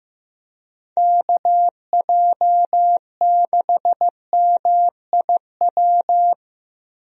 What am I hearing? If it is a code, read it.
Morse code, heard as a keyed tone, about 15 wpm, 700 Hz. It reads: KJ6MIW
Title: USA Call Signs - 6 Characters - Repeated 2x - 15wpm